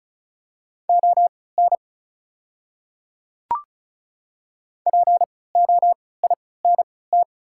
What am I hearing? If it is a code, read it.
Morse code, heard as a keyed tone, about 35 wpm, spaced as for 20 wpm, 690 Hz. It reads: ON POINT
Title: 3x Character Spacing - Top 200 Words - 35wpm